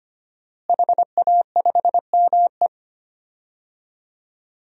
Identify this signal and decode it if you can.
Morse code, heard as a keyed tone, about 25 wpm, 700 Hz. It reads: HA5ME
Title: DXCC-100 Call Signs - 5 Characters - Repeated 3x - 25wpm